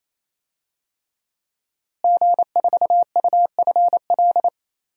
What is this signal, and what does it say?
Morse code, heard as a keyed tone, about 28 wpm, 700 Hz. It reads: G4UFL
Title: DXCC-100 Call Signs - 5 Characters - Repeated 3x - 28wpm